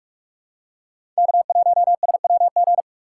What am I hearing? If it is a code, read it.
Morse code, heard as a keyed tone, about 41 wpm, 700 Hz. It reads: K1SWG
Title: USA Call Signs - 5 Characters - 4x Character Spacing - 45wpm